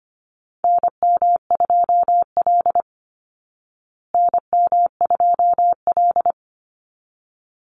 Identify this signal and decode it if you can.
Morse code, heard as a keyed tone, about 25 wpm, 700 Hz. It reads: NM2L NM2L
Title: Call Signs Repeated 3x - 25wpm